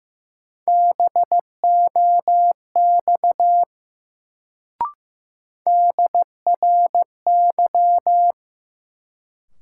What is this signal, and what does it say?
Morse code, heard as a keyed tone, about 15 wpm, 700 Hz. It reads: BOX DRY
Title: Sentences from Top 500 Words - New Word Drill - 15wpm